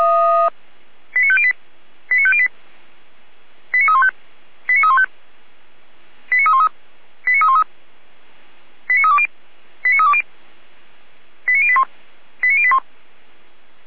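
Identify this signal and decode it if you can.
Five-tone selective call (ZVEI), heard as quick runs of five tones, tones of about 70 ms. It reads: 89508 89508 89315 89315 89313 89313 89320 89320 89081 89081
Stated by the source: Five-tone ZVEI selective call 89305, received by SDR, fire brigade Orlamünde.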